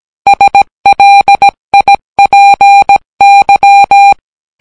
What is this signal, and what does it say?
Morse code, heard as a keyed tone, about 18 wpm, 800 Hz. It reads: SLIPY